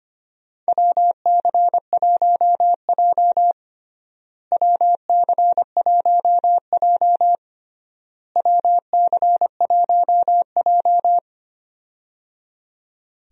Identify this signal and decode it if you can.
Morse code, heard as a keyed tone, about 25 wpm, 700 Hz. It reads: WC1J WC1J WC1J
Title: USA Call Signs - 4 Characters - Repeated 3x - 25wpm